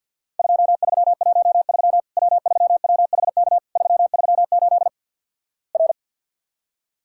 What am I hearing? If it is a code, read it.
Morse code, heard as a keyed tone, about 50 wpm, 700 Hz. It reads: 2313 W3WHK 338 R
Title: Worked All Europe DX Contest - QTC Exchange - 50wpm